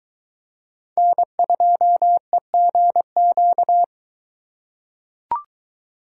Morse code, heard as a keyed tone, about 21 wpm, 700 Hz. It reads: N2EGQ